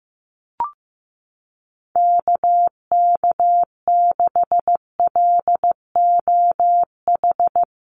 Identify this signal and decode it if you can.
Morse code, heard as a keyed tone, about 14 wpm, 700 Hz. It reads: KK6LOH
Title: USA Call Signs Repeated 2x 15wpm